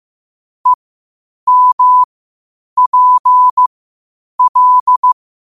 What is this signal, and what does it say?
Morse code, heard as a keyed tone, about 15 wpm, spaced as for 8 wpm, 1000 Hz. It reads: EMPL
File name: qotdfiles / 8wpm